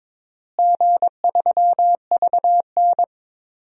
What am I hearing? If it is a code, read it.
Morse code, heard as a keyed tone, about 22 wpm, 700 Hz. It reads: G3VN
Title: DXCC-100 Call Signs - 4 Characters - Repeated 2x - 22wpm